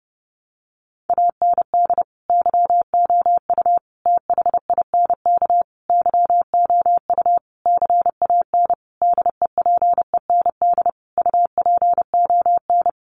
Morse code, heard as a keyed tone, about 30 wpm, 700 Hz. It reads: AND YOU THINK YOU CAN DEPEND UPON
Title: Sentences from Top 1300 Words 30wpm